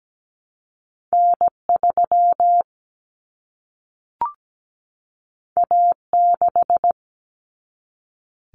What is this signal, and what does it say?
Morse code, heard as a keyed tone, about 17 wpm, 700 Hz. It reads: N3 A6
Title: Common Cut Numbers - 2-Digit 17wpm